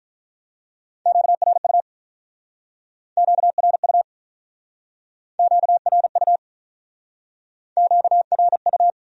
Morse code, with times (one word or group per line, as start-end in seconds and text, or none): 1.1-1.8 QRU
3.2-4.0 QRU
5.4-6.4 QRU
7.8-8.9 QRU